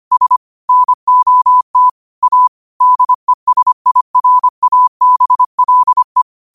Morse code, heard as a keyed tone, about 25 wpm, 1000 Hz. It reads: S NOT A DESIRABLE
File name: qotdfiles / 25wpm